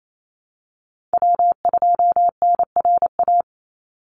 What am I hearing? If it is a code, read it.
Morse code, heard as a keyed tone, about 28 wpm, 700 Hz. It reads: W2NRA
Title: Call Signs Repeated 2x - 28wpm